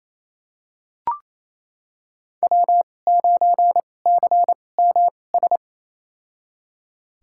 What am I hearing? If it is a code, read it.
Morse code, heard as a keyed tone, about 28 wpm, spaced as for 20 wpm, 700 Hz. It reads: W9CMS
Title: USA Call Signs - 5 Characters - 2x Character Spacing - 28wpm